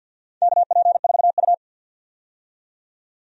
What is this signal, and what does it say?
Morse code, heard as a keyed tone, about 51 wpm, 700 Hz. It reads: KP4U
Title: DXCC-100 Call Signs - 4 Characters - 50wpm